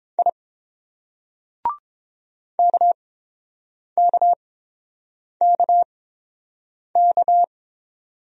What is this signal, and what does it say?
Morse code, heard as a keyed tone, about 28 wpm, spaced as for 7 wpm, 690 Hz. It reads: I KKKK